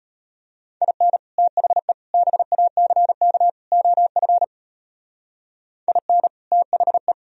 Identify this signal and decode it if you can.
Morse code, heard as a keyed tone, about 31 wpm, 700 Hz. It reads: IN THE BACK OF IN THE
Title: Speed-Racing - Sets of 4 Words 25wpm